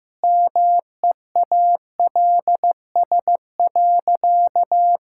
Morse code, heard as a keyed tone, about 16 wpm, 700 Hz. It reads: MEALS.
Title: Pangram Challenge v2 - 15wpm